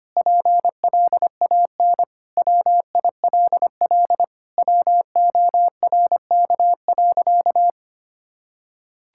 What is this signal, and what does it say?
Morse code, heard as a keyed tone, about 25 wpm, 700 Hz. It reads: PLAN WILL WORK.
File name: practice / 25wpm / Sentences with 4-Word Phrases - 25wpm